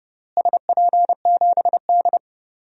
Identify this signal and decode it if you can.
Morse code, heard as a keyed tone, about 30 wpm, 700 Hz. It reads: SP7D
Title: DXCC-100 Call Signs - 4 Characters - Repeated 2x - 30wpm